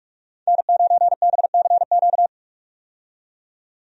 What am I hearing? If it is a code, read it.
Morse code, heard as a keyed tone, about 45 wpm, 700 Hz. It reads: N9BCQ
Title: USA Call Signs - 5 Characters - 45wpm